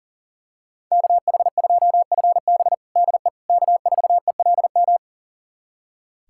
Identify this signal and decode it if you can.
Morse code, heard as a keyed tone, about 40 wpm, 700 Hz. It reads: KH2FB DE K4ELM
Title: USA Callsign DE Callsign - 5 Characters - 40wpm